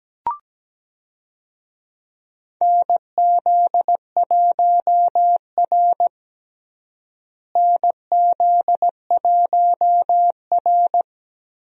Morse code, heard as a keyed tone, about 17 wpm, 700 Hz. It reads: NZ1R NZ1R
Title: USA Call Signs - 4 Characters - Repeated 3x - 17wpm